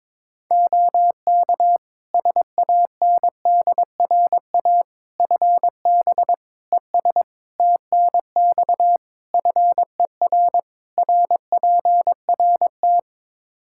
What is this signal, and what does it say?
Morse code, heard as a keyed tone, about 22 wpm, 700 Hz. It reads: OK SANDRA FB ES TNX FER RPRT
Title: Practice QSO - Exchange 2 - Info - 22wpm